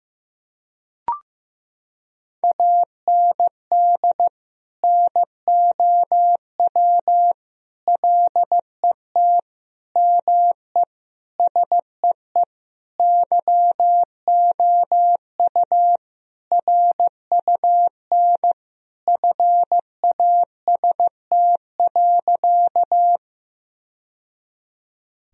Morse code, heard as a keyed tone, about 15 wpm, 700 Hz. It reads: AND NOW LET ME SEE YOU RUN FAST.
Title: Sentences from Top 1000 Words - Review Part 2 - 15wpm